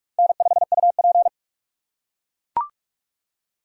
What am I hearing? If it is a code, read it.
Morse code, heard as a keyed tone, about 42 wpm, 700 Hz. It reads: N5UP
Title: USA Call Signs - 4 Characters - 3x Character Spacing - 45wpm